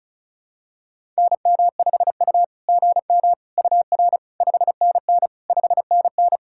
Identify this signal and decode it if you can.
Morse code, heard as a keyed tone, about 34 wpm, 660 Hz. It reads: NM5U GM UR 5NN 5NN
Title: SOTA QSO Practice 35wpm